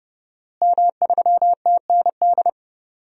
Morse code, heard as a keyed tone, about 30 wpm, 700 Hz. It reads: M3TND
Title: DXCC-100 Call Signs - 5 Characters - Repeated 2x - 30wpm